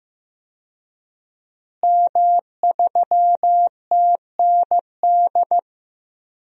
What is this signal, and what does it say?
Morse code, heard as a keyed tone, about 15 wpm, 700 Hz. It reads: M3TND
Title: DXCC-100 Call Signs - 5 Characters - Repeated 2x - 15wpm